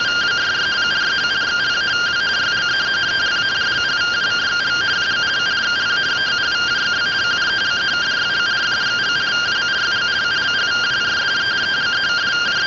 Two-tone teleprinter signal, about 170 Hz shift.